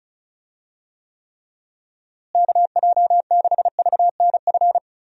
Morse code, heard as a keyed tone, about 36 wpm, 700 Hz. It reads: KJ6VNF
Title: USA Call Signs - 6 Characters - Repeated 3x - 35wpm